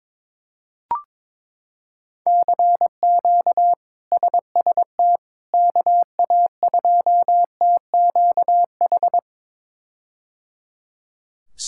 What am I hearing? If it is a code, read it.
Morse code, heard as a keyed tone, about 22 wpm, 700 Hz. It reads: CQ SST KA2TQH
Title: K1USN SST QSO Practice - 22wpm